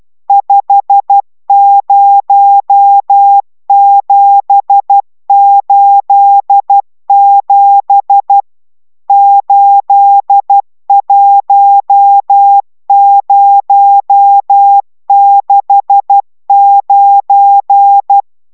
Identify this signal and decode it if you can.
Morse code, heard as a keyed tone, about 12 wpm, 800 Hz. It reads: 50787 81069